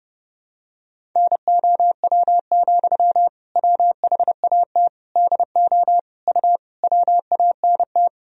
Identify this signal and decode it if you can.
Morse code, heard as a keyed tone, about 30 wpm, 700 Hz. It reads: NOW, WHAT DO U WANT
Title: Sentences from Top 200 Words and Abbreviations 30wpm